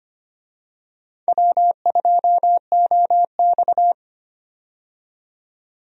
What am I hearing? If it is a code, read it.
Morse code, heard as a keyed tone, about 25 wpm, 700 Hz. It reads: W2OX
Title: USA Call Signs - 4 Characters - 25wpm